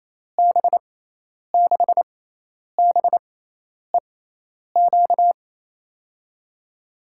Morse code, heard as a keyed tone, about 28 wpm, 700 Hz. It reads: B 6 B E Q
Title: USA Call Signs - 6x Character Spacing - 28wpm